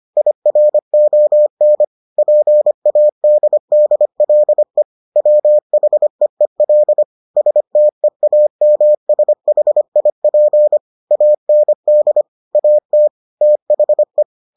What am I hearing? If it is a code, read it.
Morse code, heard as a keyed tone, about 25 wpm, 600 Hz. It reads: IRON PADDLE WHEEL STEAMSHIP AND AT THE